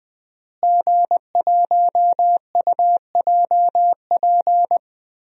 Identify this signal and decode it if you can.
Morse code, heard as a keyed tone, about 20 wpm, 700 Hz. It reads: G1UJP
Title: DXCC-100 Call Signs - 5 Characters - Repeated 3x - 20wpm